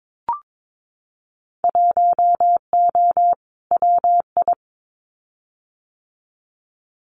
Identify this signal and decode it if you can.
Morse code, heard as a keyed tone, about 22 wpm, 700 Hz. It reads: 1O WI
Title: Winter Field Day Exchanges 22wpm